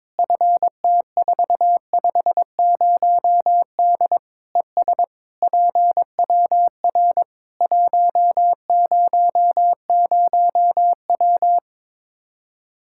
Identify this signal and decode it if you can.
Morse code, heard as a keyed tone, about 22 wpm, 700 Hz. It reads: FT450D ES PWR 100W
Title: Practice QSO - Exchange 2 - Info - 22wpm